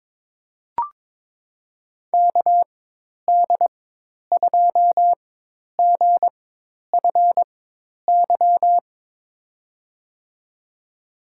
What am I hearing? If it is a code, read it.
Morse code, heard as a keyed tone, about 22 wpm, spaced as for 10 wpm, 700 Hz. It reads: KD2GFY
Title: USA Call Signs - 6 Characters - 4x Character Spacing - 22wpm